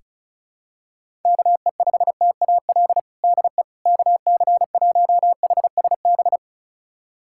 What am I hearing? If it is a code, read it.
Morse code, heard as a keyed tone, about 36 wpm, 700 Hz. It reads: KE5TAL DE KC1HSB
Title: USA Callsign DE Callsign - 6 Characters - 35wpm